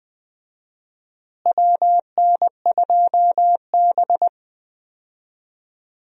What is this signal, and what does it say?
Morse code, heard as a keyed tone, about 20 wpm, 700 Hz. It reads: WN2B
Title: USA Call Signs - 4 Characters - Repeated 3x - 20wpm